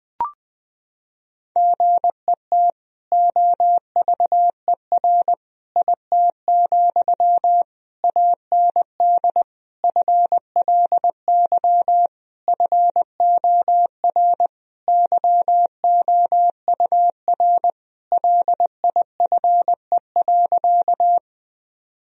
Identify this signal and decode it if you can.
Morse code, heard as a keyed tone, about 20 wpm, 700 Hz. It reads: GET OVER IT, AND FLY FOR YOUR LIFE.
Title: Sentences from Top 1000 Words - Review Part 2 - 20wpm